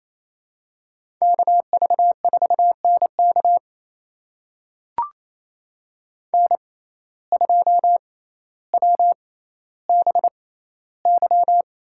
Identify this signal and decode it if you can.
Morse code, heard as a keyed tone, about 28 wpm, spaced as for 14 wpm, 700 Hz. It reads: KV4NK N2WBY
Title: USA Call Signs - 5 Characters - 6x Character Spacing - 28wpm